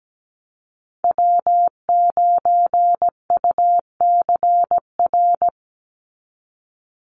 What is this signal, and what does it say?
Morse code, heard as a keyed tone, about 17 wpm, 700 Hz. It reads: W9UCR
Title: Call Signs Repeated 3x - 17wpm